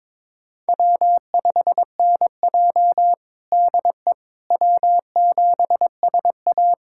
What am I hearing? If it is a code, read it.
Morse code, heard as a keyed tone, about 23 wpm, 700 Hz. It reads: W5NJ DE W7SA
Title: USA Callsign DE Callsign - 4 Characters - 22wpm